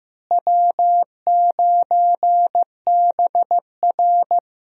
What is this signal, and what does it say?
Morse code, heard as a keyed tone, about 15 wpm, 700 Hz. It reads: W9BR
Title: USA Call Signs - 4 Characters - Repeated 3x - 15wpm